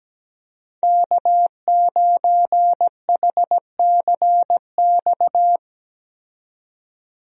Morse code, heard as a keyed tone, about 17 wpm, 700 Hz. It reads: K9HCX
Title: USA Call Signs - 5 Characters - 17wpm